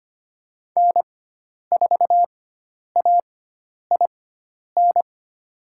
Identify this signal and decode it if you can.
Morse code, heard as a keyed tone, about 25 wpm, spaced as for 10 wpm, 700 Hz. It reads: N4AIN
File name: Lesson 16g - Letter c - Call Signs 25wpm